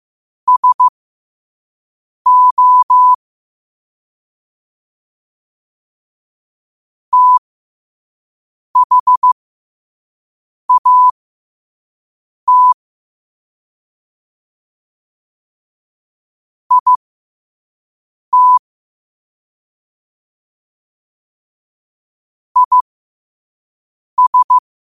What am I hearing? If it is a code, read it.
Morse code, heard as a keyed tone, about 15 wpm, spaced as for 5 wpm, 1000 Hz. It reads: SO THAT IT IS
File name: qotdfiles / 5wpm